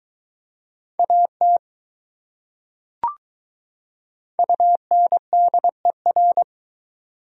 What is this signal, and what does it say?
Morse code, heard as a keyed tone, about 22 wpm, 700 Hz. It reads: AT UNDER